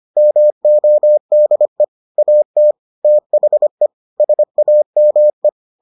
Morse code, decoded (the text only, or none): MODE AT THE SAME